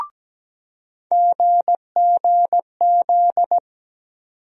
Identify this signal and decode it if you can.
Morse code, heard as a keyed tone, about 17 wpm, 700 Hz. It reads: GGZ
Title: USA Call Sign Postfixes - 3 Letters 17wpm